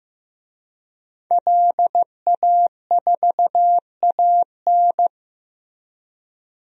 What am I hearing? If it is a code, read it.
Morse code, heard as a keyed tone, about 16 wpm, 700 Hz. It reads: LA4AN
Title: DXCC-100 Call Signs - 5 Characters - Repeated 2x - 15wpm